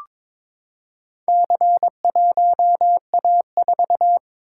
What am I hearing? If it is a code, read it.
Morse code, heard as a keyed tone, about 22 wpm, 700 Hz. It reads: C1A4
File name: practice / 22wpm / Four Letter-Number Combinations - 22wpm